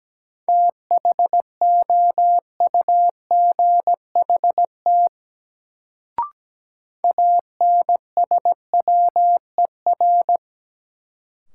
Morse code, heard as a keyed tone, about 17 wpm, 700 Hz. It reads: THOUGHT ANSWER
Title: Sentences from Top 300 Words - New Word Drill - 17wpm